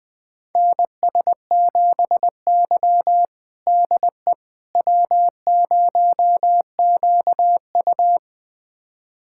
Morse code, heard as a keyed tone, about 20 wpm, 700 Hz. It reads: NS7Y DE W0QU
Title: USA Callsign DE Callsign - 4 Characters - 20wpm